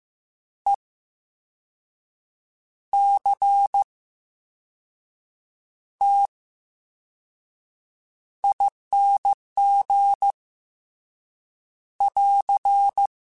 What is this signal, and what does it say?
Morse code, heard as a keyed tone, about 15 wpm, spaced as for 5 wpm, 800 Hz. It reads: ECTING <AR>
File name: Special INGs / Special INGs-mp3s / Slow ing @13